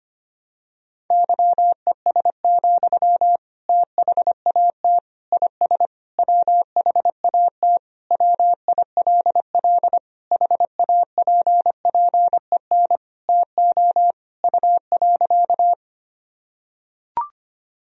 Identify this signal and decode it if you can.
Morse code, heard as a keyed tone, about 25 wpm, 700 Hz. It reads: YES, THAT IS WHAT WILL HAPPEN TO U.
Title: Sentences from Top 500 Words and Abbreviations 25wpm